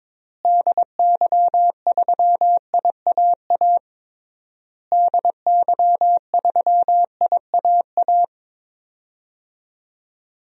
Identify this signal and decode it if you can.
Morse code, heard as a keyed tone, about 22 wpm, 700 Hz. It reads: DY3IAA DY3IAA
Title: DXCC-100 Call Signs - 6 Characters - Repeated 3x - 22wpm